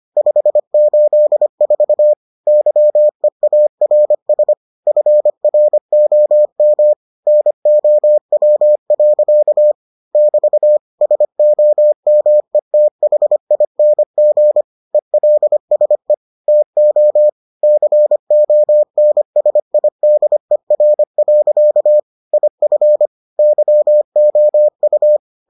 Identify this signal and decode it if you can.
Morse code, heard as a keyed tone, about 25 wpm, 600 Hz. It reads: 584 YEARS FROM NOW. <BT> SOMETHING ELSE TO CONSIDER. IF YOU